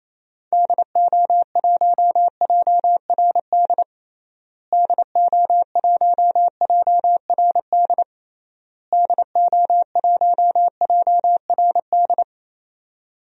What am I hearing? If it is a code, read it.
Morse code, heard as a keyed tone, about 28 wpm, 700 Hz. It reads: DO1JRD DO1JRD DO1JRD
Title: DXCC-100 Call Signs - 6 Characters - Repeated 3x - 28wpm